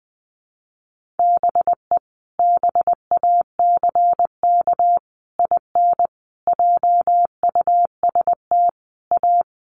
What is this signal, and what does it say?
Morse code, heard as a keyed tone, about 20 wpm, 700 Hz. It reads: BE BACK IN JUST A
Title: Sets of 5 Words Encore 20wpm